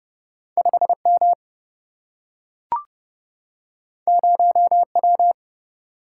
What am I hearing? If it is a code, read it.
Morse code, heard as a keyed tone, about 30 wpm, 700 Hz. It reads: 5 M 0 W
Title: Single-Letter Single-Number 30wpm